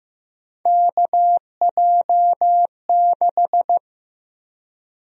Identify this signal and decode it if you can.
Morse code, heard as a keyed tone, about 15 wpm, 700 Hz. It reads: KJ6
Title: USA Call Sign Prefixes - 15wpm